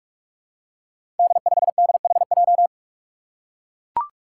Morse code, heard as a keyed tone, about 44 wpm, 700 Hz. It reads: D5DHJ
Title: Five Challenging Letter-Number Combinations 45wpm